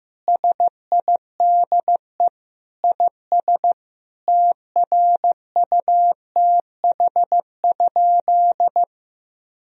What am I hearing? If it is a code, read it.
Morse code, heard as a keyed tone, about 16 wpm, 700 Hz. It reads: SIDE IS TRUTH?
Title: Sentences from Top 1000 Words - Review Part 1 - 15wpm